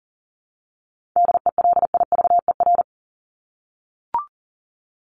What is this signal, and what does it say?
Morse code, heard as a keyed tone, about 40 wpm, 700 Hz. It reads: DELIVER
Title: Top 1751-2000 Words 40wpm